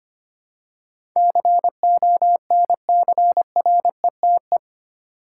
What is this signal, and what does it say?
Morse code, heard as a keyed tone, about 25 wpm, 700 Hz. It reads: CONCRETE
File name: Eight Letter Words - 25wpm